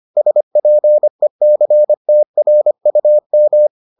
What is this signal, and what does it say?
Morse code, heard as a keyed tone, about 25 wpm, 600 Hz. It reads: SPECTRUM